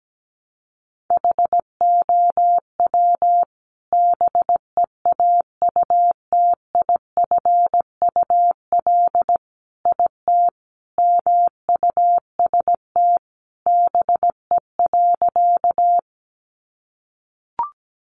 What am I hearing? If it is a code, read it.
Morse code, heard as a keyed tone, about 17 wpm, 700 Hz. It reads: HOW BEAUTIFUL IT MUST BE.
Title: Sentences from Top 900 Words 17wpm